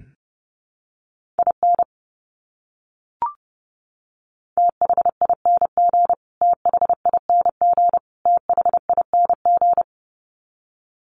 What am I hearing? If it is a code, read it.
Morse code, heard as a keyed tone, about 30 wpm, 700 Hz. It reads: IN THING THING THING
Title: Top 200 Words Repeated 30wpm